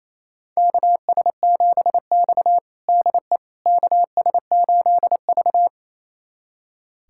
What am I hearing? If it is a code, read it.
Morse code, heard as a keyed tone, about 28 wpm, 700 Hz. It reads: KS7X DE KS8V
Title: USA Callsign DE Callsign - 4 Characters - 28wpm